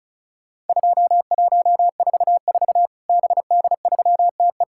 Morse code, heard as a keyed tone, about 36 wpm, 700 Hz. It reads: 2144 BD3TE
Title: Worked All Europe DX Contest - QTC Exchange - 35wpm